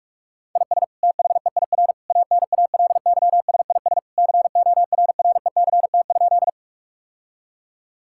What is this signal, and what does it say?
Morse code, heard as a keyed tone, about 46 wpm, 700 Hz. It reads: IS THEIR ANALYSIS CORRECT?
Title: Sentences from Top 1000 Words - Review Part 5 - 45wpm